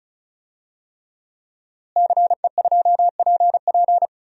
Morse code, heard as a keyed tone, about 36 wpm, 700 Hz. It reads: CE2PP
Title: DXCC-100 Call Signs - 5 Characters - Repeated 3x - 35wpm